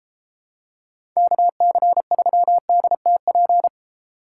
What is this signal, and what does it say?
Morse code, heard as a keyed tone, about 31 wpm, 700 Hz. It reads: KC3DTP